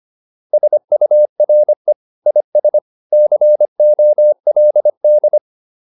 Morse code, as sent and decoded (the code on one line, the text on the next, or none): ... ..- .-. / .. ... / -.-. --- .-.. -..
SURE IS COLD